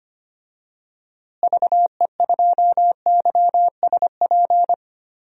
Morse code, heard as a keyed tone, about 25 wpm, 700 Hz. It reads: VE2YSP
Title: DXCC-100 Call Signs - 6 Characters - Repeated 2x - 25wpm